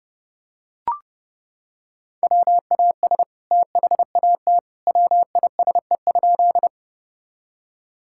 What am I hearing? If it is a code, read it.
Morse code, heard as a keyed tone, about 30 wpm, 700 Hz. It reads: WAS THAT WISE?
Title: Sentences from Top 1600 Words 30wpm